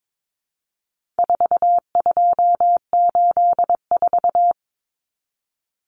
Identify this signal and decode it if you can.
Morse code, heard as a keyed tone, about 22 wpm, 700 Hz. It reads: 4284
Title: Common Cut Numbers - 4-Digit 22wpm